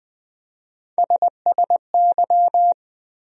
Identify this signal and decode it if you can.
Morse code, heard as a keyed tone, about 20 wpm, 700 Hz. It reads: SSY
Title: USA Call Sign Postfixes - 3 Letters 20wpm